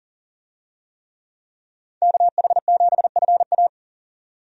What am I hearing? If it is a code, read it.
Morse code, heard as a keyed tone, about 40 wpm, 700 Hz. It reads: KH7FA
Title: USA Call Signs - 5 Characters - 40wpm